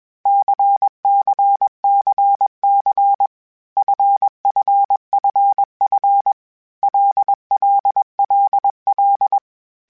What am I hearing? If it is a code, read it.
Morse code, heard as a keyed tone, about 21 wpm, 800 Hz. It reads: CCCC FFFF LLLL
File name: Basic Sets / Basic Letter Groups Collective7 / LFCYQ@20